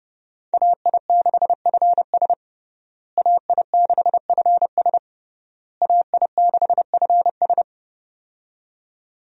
Morse code, heard as a keyed tone, about 30 wpm, 700 Hz. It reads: AI6FS AI6FS AI6FS
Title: USA Call Signs - 5 Characters - Repeated 3x - 30wpm